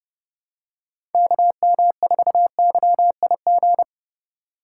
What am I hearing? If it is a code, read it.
Morse code, heard as a keyed tone, about 30 wpm, 700 Hz. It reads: KM4YIG